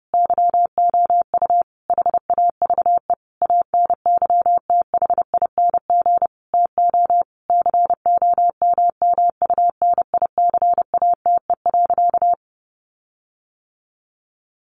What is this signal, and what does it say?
Morse code, heard as a keyed tone, about 30 wpm, 700 Hz. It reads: YOU HAVE ANYTHING TO COMMUNICATE.
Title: Sentences from Top 1300 Words 30wpm